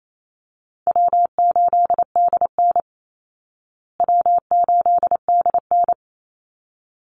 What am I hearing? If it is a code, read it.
Morse code, heard as a keyed tone, about 28 wpm, 700 Hz. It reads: W8DN W8DN
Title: Easy Call Signs Repeated 2x - 28wpm